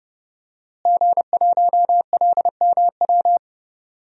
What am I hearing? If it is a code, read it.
Morse code, heard as a keyed tone, about 30 wpm, 700 Hz. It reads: G1LMW
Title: DXCC-100 Call Signs - 5 Characters - Repeated 2x - 30wpm